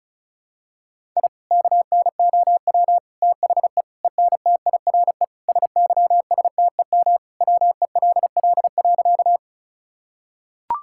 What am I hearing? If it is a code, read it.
Morse code, heard as a keyed tone, about 36 wpm, 700 Hz. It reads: I KNOW THE ENTIRE SYSTEM WELL.
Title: Sentences from Top 1000 Words - Review Part 5 - 35wpm